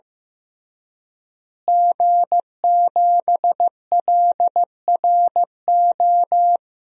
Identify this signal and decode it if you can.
Morse code, heard as a keyed tone, about 15 wpm, 700 Hz. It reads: G7LRO
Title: DXCC-100 Call Signs - 5 Characters - Repeated 3x - 15wpm